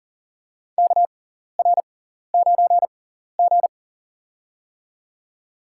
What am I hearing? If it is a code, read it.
Morse code, heard as a keyed tone, about 38 wpm, spaced as for 14 wpm, 700 Hz. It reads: KR9G
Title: USA Call Signs - 4 Characters - 6x Character Spacing - 40wpm